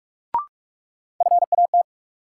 Morse code, heard as a keyed tone, about 46 wpm, 700 Hz. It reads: FAT
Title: Sentences from Top 600 Words - New Word Drill - 45wpm